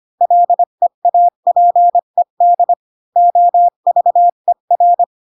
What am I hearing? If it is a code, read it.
Morse code, heard as a keyed tone, about 25 wpm, 700 Hz. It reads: LEAPED OVER
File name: Pinocchio / Pinocchio ch4 20 25 ik1hge